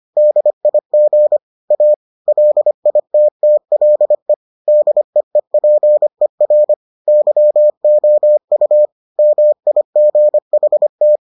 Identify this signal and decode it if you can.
Morse code, heard as a keyed tone, about 25 wpm, 600 Hz. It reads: DIG A LITTLE DEEPER YOU MIGHT